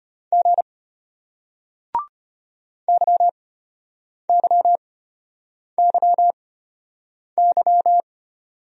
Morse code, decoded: G YYYY

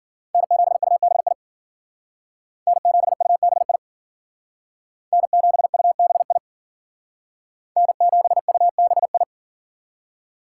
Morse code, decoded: N7UBI N7UBI N7UBI N7UBI

